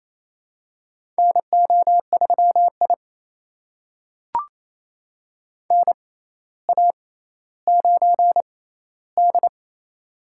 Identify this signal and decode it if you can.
Morse code, heard as a keyed tone, about 28 wpm, spaced as for 14 wpm, 700 Hz. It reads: NO3I NA9D
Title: USA Call Signs - 4 Characters - 6x Character Spacing - 28wpm